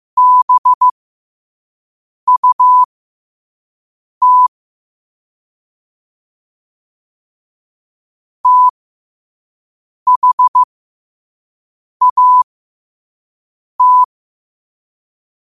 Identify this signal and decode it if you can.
Morse code, heard as a keyed tone, about 15 wpm, spaced as for 5 wpm, 1000 Hz. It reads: BUT THAT
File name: qotdfiles / 5wpm